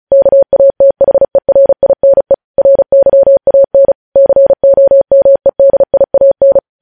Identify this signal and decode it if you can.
Morse code, heard as a keyed tone, about 36 wpm, 550 Hz. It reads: KATHERINE RYAN COMEDIAN
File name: podcast / News headlines-1x-35x35